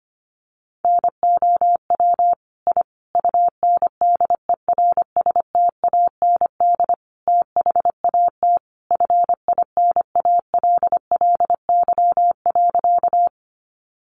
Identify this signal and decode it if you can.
Morse code, heard as a keyed tone, about 25 wpm, 700 Hz. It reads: NOW I UNDERSTAND THAT FINALLY.
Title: Sentences from Top 1000 Words 25wpm